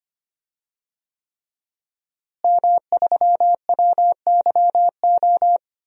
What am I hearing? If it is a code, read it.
Morse code, heard as a keyed tone, about 25 wpm, 700 Hz. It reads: M3WYO
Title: DXCC-100 Call Signs - 5 Characters - Repeated 3x - 25wpm